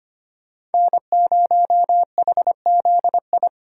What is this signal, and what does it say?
Morse code, heard as a keyed tone, about 33 wpm, 700 Hz. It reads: N0HZI